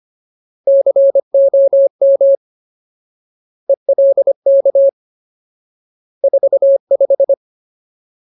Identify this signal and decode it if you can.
Morse code, heard as a keyed tone, about 25 wpm, 550 Hz. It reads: COM ELK 45
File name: Ultimate Word List / 25WPM Ultimate